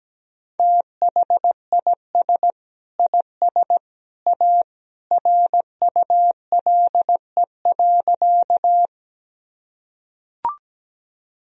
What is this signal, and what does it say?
Morse code, heard as a keyed tone, about 17 wpm, 700 Hz. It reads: THIS IS A RULE.
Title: Sentences from Top 1000 Words - Review Part 2 - 17wpm